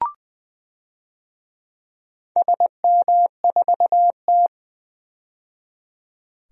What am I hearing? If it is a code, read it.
Morse code, heard as a keyed tone, about 20 wpm, 700 Hz. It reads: SM4T
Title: DXCC-100 Call Signs - 4 Characters - 20wpm